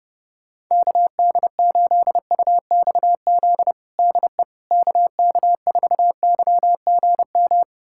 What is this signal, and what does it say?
Morse code, heard as a keyed tone, about 30 wpm, 700 Hz. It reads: KD8UXZ DE KK4YGM
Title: USA Callsign DE Callsign - 6 Characters - 30wpm